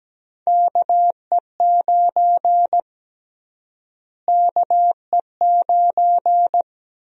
Morse code, heard as a keyed tone, about 17 wpm, 700 Hz. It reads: KE9 KE9
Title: USA Call Sign Prefixes - Repeated 2x - 17wpm